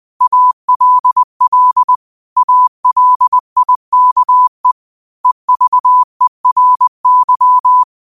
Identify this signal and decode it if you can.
Morse code, heard as a keyed tone, about 20 wpm, 1000 Hz. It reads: ALL ALIKE EVERY
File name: qotdfiles / 20wpm